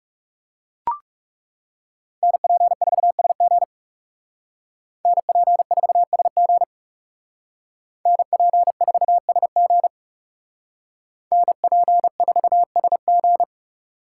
Morse code, heard as a keyed tone, about 35 wpm, 700 Hz. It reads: NP4SG NP4SG NP4SG NP4SG
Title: Speed Racing - USA 2x2 Call Signs 30wpm